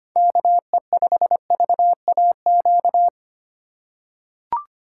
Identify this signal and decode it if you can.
Morse code, heard as a keyed tone, about 25 wpm, 700 Hz. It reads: KE5VAQ
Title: USA Call Signs - 6 Characters - 4x Character Spacing - 25wpm